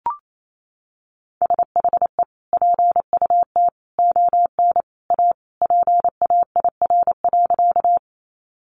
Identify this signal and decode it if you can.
Morse code, heard as a keyed tone, about 28 wpm, 700 Hz. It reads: SHE PUT ON A PAIR.
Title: Sentences from Top 1200 Words 28wpm